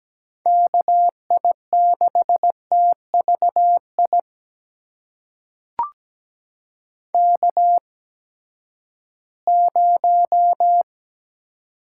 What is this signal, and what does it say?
Morse code, heard as a keyed tone, about 17 wpm, spaced as for 12 wpm, 700 Hz. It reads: KI6TVI K0
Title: USA Call Signs - 8x Character Spacing - 17wpm